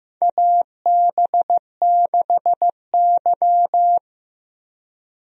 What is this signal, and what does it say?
Morse code, heard as a keyed tone, about 15 wpm, 700 Hz. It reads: AB6Y
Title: USA Call Signs - 4 Characters - Repeated 2x - 15wpm